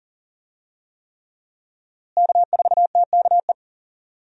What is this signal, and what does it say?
Morse code, heard as a keyed tone, about 40 wpm, 700 Hz. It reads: K4TKE
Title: USA Call Signs - 5 Characters - Repeated 3x - 40wpm